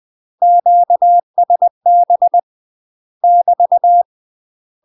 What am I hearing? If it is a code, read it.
Morse code, heard as a keyed tone, about 20 wpm, 700 Hz. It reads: QSB <BT>
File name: UK MASTER QSOs TRAINING / 20WPM / UK MASTER QSOs 0040